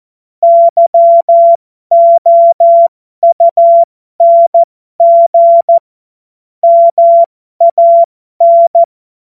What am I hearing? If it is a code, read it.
Morse code, heard as a keyed tone, about 14 wpm, spaced as for 12 wpm, 680 Hz. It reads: YOUNG MAN
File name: Morsekurs / Witze